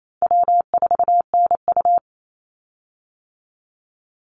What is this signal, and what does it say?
Morse code, heard as a keyed tone, about 28 wpm, 700 Hz. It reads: W4NU
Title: Call Signs - 28wpm